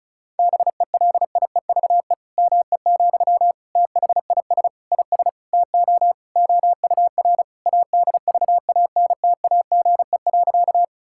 Morse code, heard as a keyed tone, about 34 wpm, 700 Hz. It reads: BELIEVE ME, THIS IS TO OUR ADVANTAGE.
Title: Sentences from Top 1000 Words - Review Part 6 - 35wpm